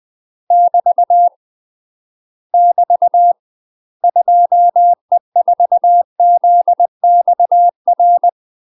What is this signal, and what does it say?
Morse code, heard as a keyed tone, about 20 wpm, 700 Hz. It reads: <BT> <BT> 2E4ZXR
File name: UK MASTER QSOs TRAINING / 20WPM / UK MASTER QSOs 0046